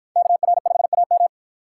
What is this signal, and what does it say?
Morse code, heard as a keyed tone, about 53 wpm, 700 Hz. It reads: KR5AM